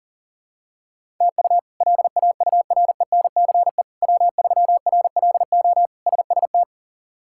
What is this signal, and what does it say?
Morse code, heard as a keyed tone, about 40 wpm, 700 Hz. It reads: TU LAURENCE W3RLO SST
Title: K1USN SST QSO Practice - 40wpm